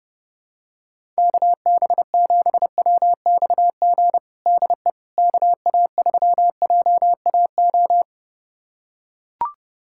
Morse code, heard as a keyed tone, about 30 wpm, 700 Hz. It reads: KB7WXG DE KA3JAO
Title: USA Callsign DE Callsign 30wpm